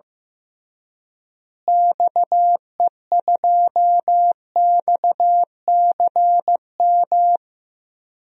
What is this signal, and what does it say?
Morse code, heard as a keyed tone, about 15 wpm, 700 Hz. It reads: XE2XCM
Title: DXCC-100 Call Signs - 6 Characters - Repeated 3x - 15wpm